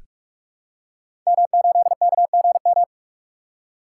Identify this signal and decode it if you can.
Morse code, heard as a keyed tone, about 44 wpm, 700 Hz. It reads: M8KGM
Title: Code Groups - 5 Letter:Numbers - 45wpm